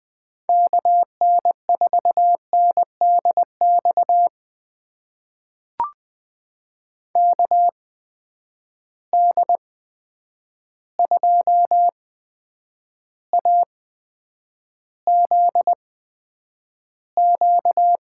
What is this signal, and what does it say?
Morse code, heard as a keyed tone, about 20 wpm, spaced as for 9 wpm, 700 Hz. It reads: KN4NDX KD2AZQ
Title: USA Call Signs - 6 Characters - 8x Character Spacing - 20wpm